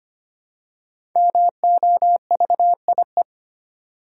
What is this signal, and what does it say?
Morse code, heard as a keyed tone, about 25 wpm, 700 Hz. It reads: MOVIE
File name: 5 Letter Words - First 3 Letters Given - 25wpm